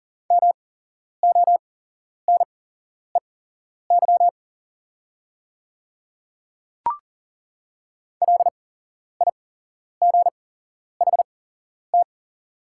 Morse code, heard as a keyed tone, about 40 wpm, spaced as for 11 wpm, 700 Hz. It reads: MONEY LIGHT
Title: 8x Character Spacing - Top 500 Words - 40wpm